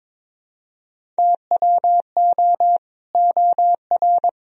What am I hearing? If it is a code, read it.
Morse code, heard as a keyed tone, about 21 wpm, 700 Hz. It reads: TWO OR
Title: Sentences from Top 1000 Words - Review Part 1 - 22wpm